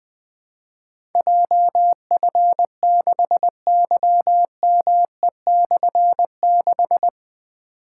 Morse code, heard as a keyed tone, about 20 wpm, 700 Hz. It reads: JF6YME/6
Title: Difficult Call Signs 20wpm